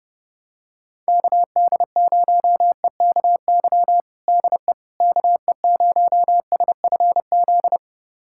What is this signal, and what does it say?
Morse code, heard as a keyed tone, about 30 wpm, 700 Hz. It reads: KD0EKY DE KE0SFZ
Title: USA Callsign DE Callsign - 6 Characters - 30wpm